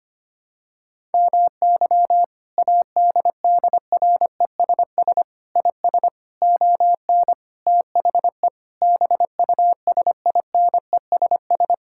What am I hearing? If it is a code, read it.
Morse code, heard as a keyed tone, about 25 wpm, 700 Hz. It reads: MY ADDRESS IS ON THE BUSINESS
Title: Sentences from Top 1000 Words - Review Part 6 - 25wpm